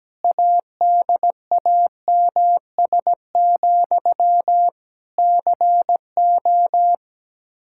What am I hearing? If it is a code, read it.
Morse code, heard as a keyed tone, about 17 wpm, 700 Hz. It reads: ADAMS, CO
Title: Practice QSO - Exchange 1 - Beginning - 17wpm